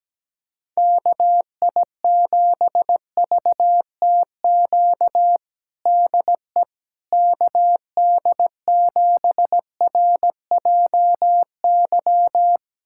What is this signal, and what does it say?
Morse code, heard as a keyed tone, about 17 wpm, 700 Hz. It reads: KI7VTQ DE KD7RJY
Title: USA Callsign DE Callsign - 6 Characters - 17wpm